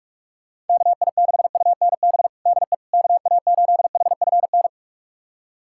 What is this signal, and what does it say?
Morse code, heard as a keyed tone, about 41 wpm, 690 Hz. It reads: KI6UNB DE KA8HFN
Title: USA Callsign DE Callsign - 6 Characters - 45wpm